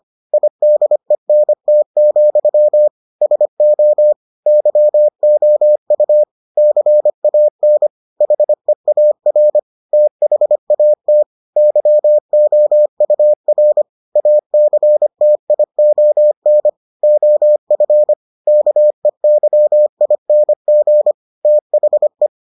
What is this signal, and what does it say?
Morse code, heard as a keyed tone, about 25 wpm, 600 Hz. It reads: IDENT, SO YOU CAN HEAR THAT YOUR ACTION OF KEYING THE